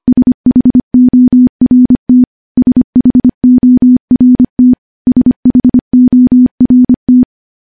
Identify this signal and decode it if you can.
Morse code, heard as a keyed tone, about 25 wpm, 250 Hz. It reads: SHORT SHORT SHORT
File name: morse / morse-training-25WPM